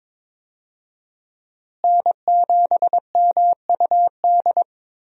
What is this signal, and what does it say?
Morse code, heard as a keyed tone, about 22 wpm, 700 Hz. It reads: N7MUD
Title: USA Call Signs - 5 Characters - Repeated 2x - 22wpm